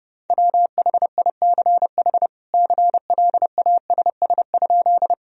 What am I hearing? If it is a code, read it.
Morse code, heard as a keyed tone, about 29 wpm, 700 Hz. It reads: WHICH CLASS?
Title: Sentences from Top 1000 Words - Review Part 2 - 30wpm